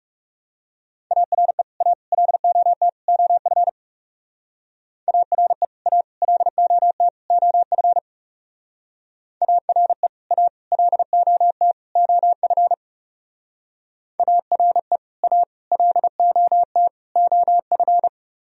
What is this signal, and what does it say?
Morse code, heard as a keyed tone, about 39 wpm, 700 Hz. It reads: ARE A LOT OF ARE A LOT OF ARE A LOT OF ARE A LOT OF